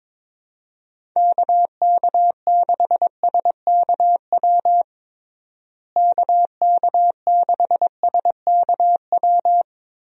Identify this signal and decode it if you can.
Morse code, heard as a keyed tone, about 22 wpm, 700 Hz. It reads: KK6SKW KK6SKW
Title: USA Call Signs - 6 Characters - Repeated 3x - 22wpm